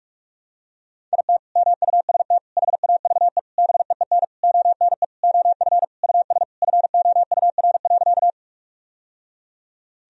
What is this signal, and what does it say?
Morse code, heard as a keyed tone, about 44 wpm, 700 Hz. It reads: IT MUST HAVE BEEN ONE OF US FOUR.
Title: Sentences from Top 1000 Words - Review Part 1 - 45wpm